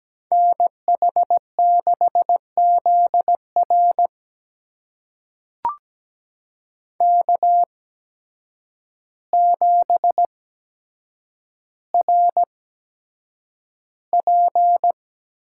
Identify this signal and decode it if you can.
Morse code, heard as a keyed tone, about 17 wpm, spaced as for 8 wpm, 700 Hz. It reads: NH6ZR K7RP
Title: USA Call Signs - 8x Character Spacing - 17wpm